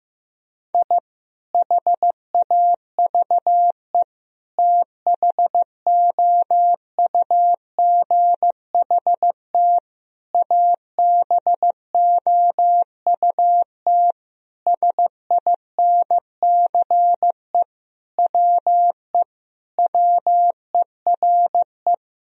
Morse code, heard as a keyed tone, about 15 wpm, 700 Hz. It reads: I HAVE THOUGHT ABOUT SINCE WE WERE
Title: Sentences from Top 1000 Words - Review Part 1 - 15wpm